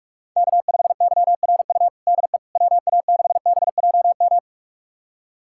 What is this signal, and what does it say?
Morse code, heard as a keyed tone, about 45 wpm, 700 Hz. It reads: K5YRU DE WA6BJM
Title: USA Callsign DE Callsign 45wpm